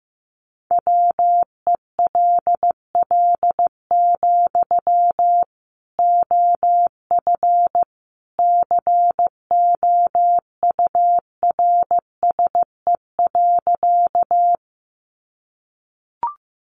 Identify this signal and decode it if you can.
Morse code, heard as a keyed tone, about 15 wpm, 700 Hz. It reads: WELL, OF COURSE.
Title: Sentences from Top 500 Words 15wpm